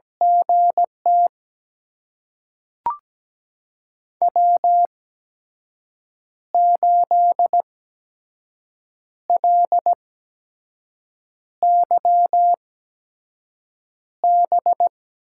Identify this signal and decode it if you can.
Morse code, heard as a keyed tone, about 17 wpm, spaced as for 5 wpm, 700 Hz. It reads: GT W8LYB
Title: USA Call Signs - 8x Character Spacing - 17wpm